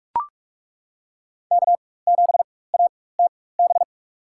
Morse code, heard as a keyed tone, about 41 wpm, spaced as for 20 wpm, 700 Hz. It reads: K7ATB
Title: USA Call Signs - 4x Character Spacing - 45wpm